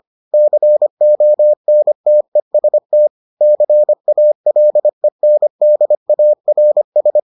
Morse code, heard as a keyed tone, about 25 wpm, 600 Hz. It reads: CONTEST CALENDARS